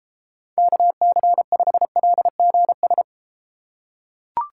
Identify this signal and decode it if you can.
Morse code, heard as a keyed tone, about 31 wpm, 700 Hz. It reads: KC5LGS